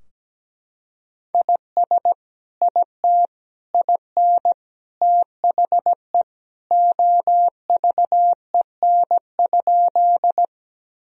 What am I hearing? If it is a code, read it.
Morse code, heard as a keyed tone, about 17 wpm, 700 Hz. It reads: IS IT IN THE OVEN?
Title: Sentences from Top 1000 Words - Review Part 4 - 17wpm